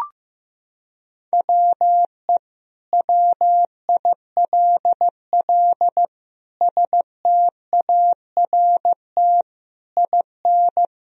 Morse code, heard as a keyed tone, about 15 wpm, 700 Hz. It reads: WE WILL START IN
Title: Sentences from Top 1000 Words - Review Part 1 - 15wpm